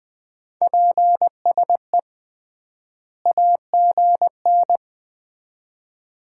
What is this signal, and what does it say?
Morse code, heard as a keyed tone, about 20 wpm, 700 Hz. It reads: PSE AGN
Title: Common QSO Phrases - 3x Word-Spacing - 20wpm